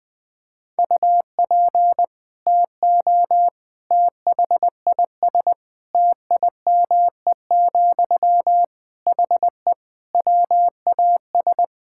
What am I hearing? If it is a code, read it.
Morse code, heard as a keyed tone, about 20 wpm, 700 Hz. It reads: UP TO THIS TIME, HE WAS
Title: Sentences from Top 1000 Words - Review Part 6 - 20wpm